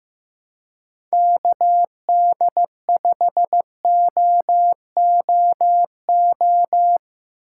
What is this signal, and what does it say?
Morse code, heard as a keyed tone, about 15 wpm, 700 Hz. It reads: KD5OOO
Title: USA Call Signs - 6 Characters - 4x Character Spacing - 15wpm